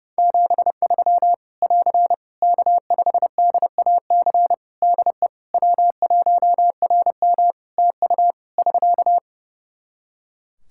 Morse code, heard as a keyed tone, about 32 wpm, 700 Hz. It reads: 73 <AR> K5DAC DE W1RM TU <SK>
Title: Practice QSO - Exchange 4 - Ending - 30wpm